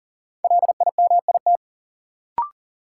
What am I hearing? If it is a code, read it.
Morse code, heard as a keyed tone, about 40 wpm, 700 Hz. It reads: LIMIT
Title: Sentences from Top 1500 Words - New Word Drill Alt - 40wpm